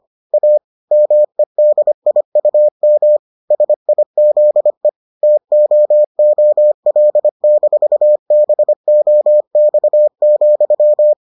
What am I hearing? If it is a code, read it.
Morse code, heard as a keyed tone, about 25 wpm, 600 Hz. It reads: A MEDIUM SIZE TOOL-BOX,